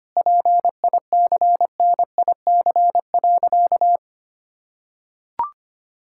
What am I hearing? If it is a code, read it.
Morse code, heard as a keyed tone, about 25 wpm, 700 Hz. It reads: PICNIC.
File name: practice / 25wpm / Difficult Sentences with Hints - 25wpm